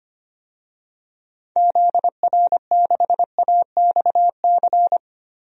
Morse code, heard as a keyed tone, about 25 wpm, 700 Hz. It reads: ZR6AXC
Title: DXCC-100 Call Signs - 6 Characters - 25wpm